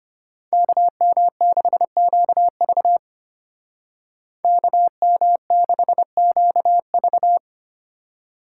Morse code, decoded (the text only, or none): KM6QV KM6QV